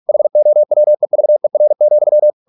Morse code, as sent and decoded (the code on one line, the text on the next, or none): .... --- .-- . ...- . .-. --..--
HOWEVER,